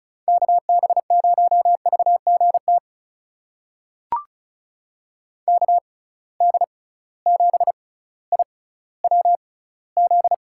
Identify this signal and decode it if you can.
Morse code, heard as a keyed tone, about 36 wpm, spaced as for 18 wpm, 700 Hz. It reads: KB0VGT KD7IWZ
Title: USA Call Signs - 6 Characters - 6x Character Spacing - 35wpm